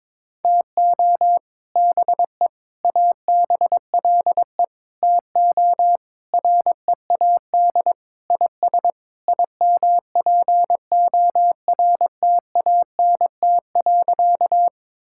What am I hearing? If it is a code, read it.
Morse code, heard as a keyed tone, about 22 wpm, 700 Hz. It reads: TO BE ABLE TO READ IS IMPORTANT.
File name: practice / 22wpm / Sentences with 4-Word Phrases - 22wpm